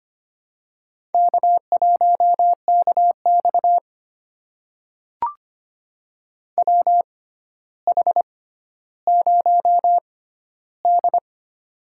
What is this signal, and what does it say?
Morse code, heard as a keyed tone, about 25 wpm, spaced as for 13 wpm, 700 Hz. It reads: K1KX WH0D
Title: USA Call Signs - 4 Characters - 6x Character Spacing - 25wpm